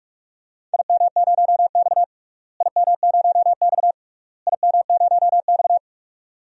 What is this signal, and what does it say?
Morse code, heard as a keyed tone, about 46 wpm, 700 Hz. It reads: IM0X IM0X IM0X
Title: DXCC-100 Call Signs - 4 Characters - Repeated 3x - 45wpm